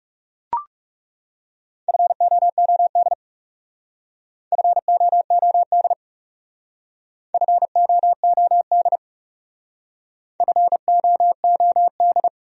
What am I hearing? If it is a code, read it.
Morse code, heard as a keyed tone, about 37 wpm, 700 Hz. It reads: FOOD FOOD FOOD FOOD
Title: Speed-Racing Top 300 Words 30wpm